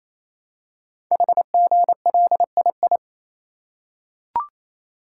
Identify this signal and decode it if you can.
Morse code, heard as a keyed tone, about 27 wpm, 700 Hz. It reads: HGLII
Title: Code Groups - 5 Letters - 28wpm